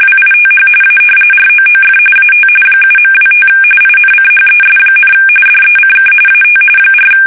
RTTY, about 45.45 baud. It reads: WITH A ROOF AT 7.25 INCHES, DECK AT 4
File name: fsk850 / battleshipnj